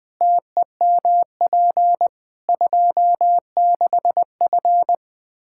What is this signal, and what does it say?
Morse code, heard as a keyed tone, about 20 wpm, 700 Hz. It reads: TEMP 26F
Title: Practice QSO - Exchange 2 - Info - 20wpm